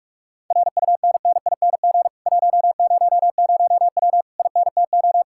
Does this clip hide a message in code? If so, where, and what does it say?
Yes: Morse code from 0.5 to 5.2 s, heard as a keyed tone, about 49 wpm, 700 Hz. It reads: RUNNING 100W INTO